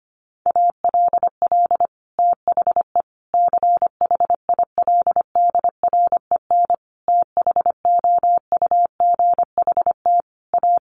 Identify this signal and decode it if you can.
Morse code, heard as a keyed tone, about 25 wpm, 700 Hz. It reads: ALL THE CHILDREN THOUGHT A
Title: Sentences from Top 800 Words 25wpm